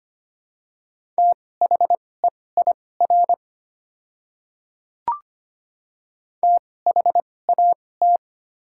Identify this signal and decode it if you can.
Morse code, heard as a keyed tone, about 25 wpm, spaced as for 18 wpm, 700 Hz. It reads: THEIR THAT
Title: 2x Character Spacing - Top 500 Words - 25wpm